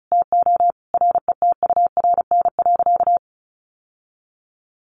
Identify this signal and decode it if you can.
Morse code, heard as a keyed tone, about 34 wpm, 700 Hz. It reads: TO RETURN.
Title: Sentences from Top 800 Words 35wpm